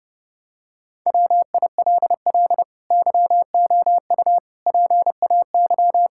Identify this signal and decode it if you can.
Morse code, heard as a keyed tone, about 30 wpm, 700 Hz. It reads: WILL YOU PAY
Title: Sentences from Top 1600 Words 30wpm